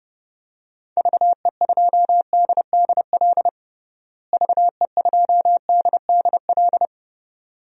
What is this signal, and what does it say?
Morse code, heard as a keyed tone, about 28 wpm, 700 Hz. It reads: VE2DDL VE2DDL
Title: DXCC-100 Call Signs - 6 Characters - Repeated 2x - 30wpm